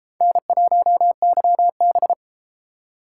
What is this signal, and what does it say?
Morse code, heard as a keyed tone, about 31 wpm, 700 Hz. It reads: N1YB